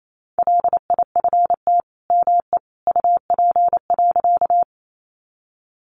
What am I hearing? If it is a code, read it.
Morse code, heard as a keyed tone, about 28 wpm, 700 Hz. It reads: LIFT ME UP.
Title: Sentences from Top 1000 Words 28wpm